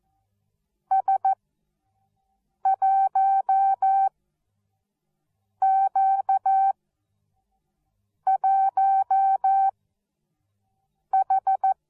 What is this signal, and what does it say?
Morse code, heard as a keyed tone, about 14 wpm, spaced as for 5 wpm, 770 Hz. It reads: S1Q1H